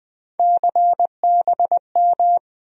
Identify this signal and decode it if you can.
Morse code, heard as a keyed tone, about 20 wpm, 700 Hz. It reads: CBM
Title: USA Call Sign Postfixes - 3 Letters 20wpm